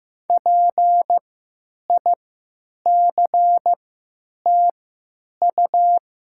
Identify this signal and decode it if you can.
Morse code, heard as a keyed tone, about 15 wpm, spaced as for 9 wpm, 700 Hz. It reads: PICTU
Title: 3x Character Spacing - Top 200 Words - 15wpm